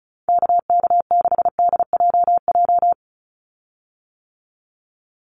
Morse code, heard as a keyed tone, about 34 wpm, 700 Hz. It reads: KK6DJJ
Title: USA Call Signs 35wpm